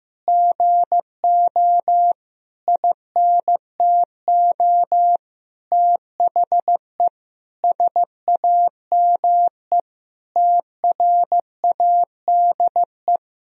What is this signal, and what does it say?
Morse code, heard as a keyed tone, about 15 wpm, 700 Hz. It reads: GO INTO THE SAME TRADE
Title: Sentences from Top 1000 Words - Review Part 4 - 15wpm